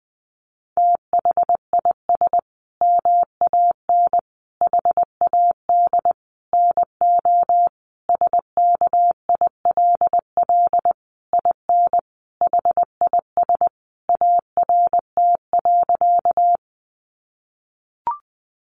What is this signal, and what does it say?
Morse code, heard as a keyed tone, about 20 wpm, 700 Hz. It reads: THIS MAN HAD NO SKILL IN HIS ART.
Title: Sentences from Top 1200 Words 20wpm